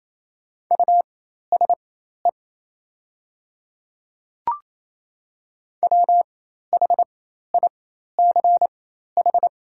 Morse code, heard as a keyed tone, about 28 wpm, spaced as for 13 wpm, 700 Hz. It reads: USE WHICH
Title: 4x Character Spacing - Top 100 Words - 28wpm